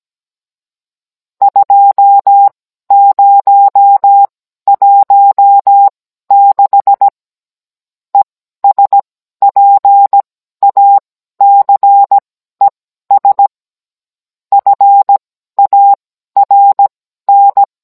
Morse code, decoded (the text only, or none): 2016 ESPACES FARN